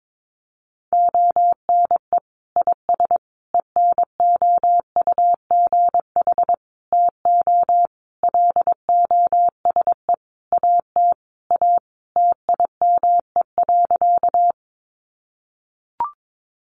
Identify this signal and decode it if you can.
Morse code, heard as a keyed tone, about 22 wpm, 700 Hz. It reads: ONE IS ENOUGH TO LOSE AT A TIME.
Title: Sentences from Top 1100 Words 22wpm